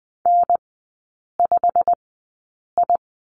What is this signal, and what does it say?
Morse code, heard as a keyed tone, about 20 wpm, spaced as for 8 wpm, 690 Hz. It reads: N5I
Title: 093 Lesson 16e - Letter c - Three Characters 20wpm